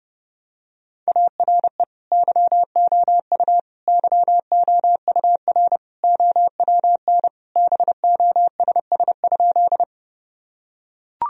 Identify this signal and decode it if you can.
Morse code, heard as a keyed tone, about 32 wpm, 700 Hz. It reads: ARE YOU YOUR OWN BOSS?
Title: Sentences from Top 1000 Words - Review Part 4 - 30wpm